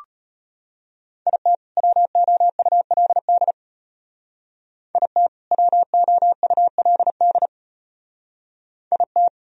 Morse code, heard as a keyed tone, about 31 wpm, 700 Hz. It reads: IT WOULD IT WOULD IT